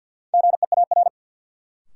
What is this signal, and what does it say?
Morse code, heard as a keyed tone, about 47 wpm, 700 Hz. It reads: GEAR
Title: Sentences from Top 1400 Words - New Word Drill - 50wpm